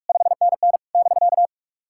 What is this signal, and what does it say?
Morse code, heard as a keyed tone, about 46 wpm, 690 Hz. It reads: 5NN <BK>
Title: SOTA QSO Practice 45wpm